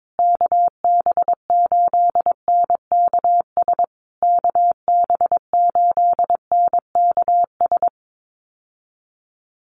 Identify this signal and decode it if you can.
Morse code, heard as a keyed tone, about 22 wpm, 690 Hz. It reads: KB8NKS KB8NKS
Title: USA Call Signs Repeated 3x 22wpm